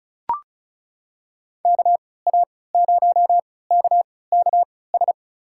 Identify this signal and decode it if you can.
Morse code, heard as a keyed tone, about 36 wpm, spaced as for 20 wpm, 700 Hz. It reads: KA0KKS
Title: USA Call Signs - 6 Characters - 3x Character Spacing - 35wpm